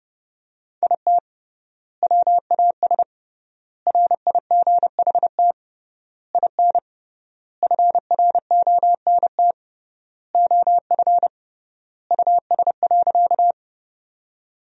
Morse code, heard as a keyed tone, about 32 wpm, 700 Hz. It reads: IT WAS RIGHT IN FRONT OF US.
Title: 3x Word-Spacing - Sentences from Top 500 Words - 30wpm